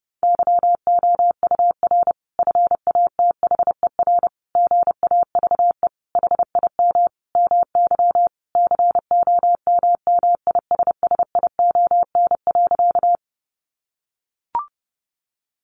Morse code, heard as a keyed tone, about 30 wpm, 700 Hz. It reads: YOUR FATHER GAVE HIM MY COMMISSION.
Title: Sentences from Top 1400 Words 30wpm